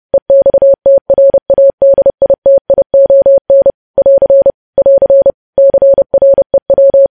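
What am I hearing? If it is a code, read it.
Morse code, heard as a keyed tone, about 30 wpm, 550 Hz. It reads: EXTRADITION <AR> <AR> CREW
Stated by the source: News headlines sent as Morse practice at 30.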